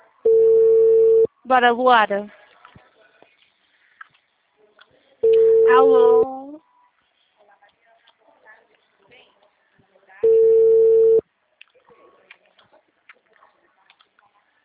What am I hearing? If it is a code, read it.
Ringback tone (China, 450 Hz), with speech.